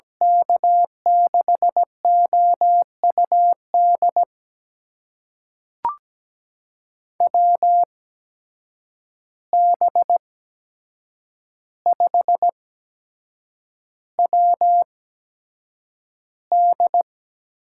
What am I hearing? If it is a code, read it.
Morse code, heard as a keyed tone, about 17 wpm, spaced as for 7 wpm, 700 Hz. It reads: K6OUD WB5WD
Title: USA Call Signs - 6 Characters - 8x Character Spacing - 17wpm